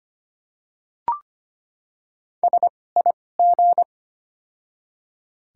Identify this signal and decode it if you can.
Morse code, heard as a keyed tone, about 25 wpm, spaced as for 18 wpm, 700 Hz. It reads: SIG
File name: Top 90 QSO Elements - 2x Character Spacing - 25wpm